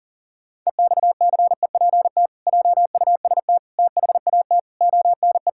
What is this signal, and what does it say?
Morse code, heard as a keyed tone, about 40 wpm, 700 Hz. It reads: EXCEPT JUST THAT ONE
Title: Sentences from Top 1600 Words 40wpm